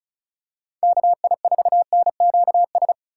Morse code, heard as a keyed tone, about 36 wpm, 700 Hz. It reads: KI4NQS
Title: USA Call Signs - 6 Characters - 2x Character Spacing - 35wpm